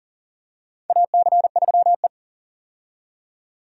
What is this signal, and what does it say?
Morse code, heard as a keyed tone, about 40 wpm, 700 Hz. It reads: AC3E
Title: USA Call Signs - 4 Characters - 40wpm